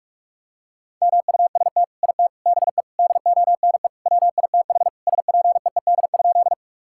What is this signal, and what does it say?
Morse code, heard as a keyed tone, about 46 wpm, 700 Hz. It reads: MUST IT BE DONE WITH SPEED?
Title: Sentences from Top 1000 Words - Review Part 5 - 45wpm